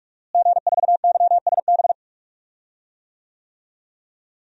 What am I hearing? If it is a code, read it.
Morse code, heard as a keyed tone, about 42 wpm, 700 Hz. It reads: G4YSB
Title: DXCC-100 Call Signs - 5 Characters - Repeated 2x - 45wpm